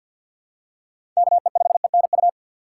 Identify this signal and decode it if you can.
Morse code, heard as a keyed tone, about 53 wpm, 700 Hz. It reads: KE5ENU
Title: USA Call Signs - 6 Characters - 4x Character Spacing - 50wpm